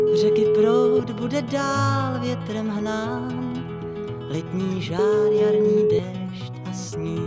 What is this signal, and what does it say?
Ringback tone (Europe, 425 Hz), with noise.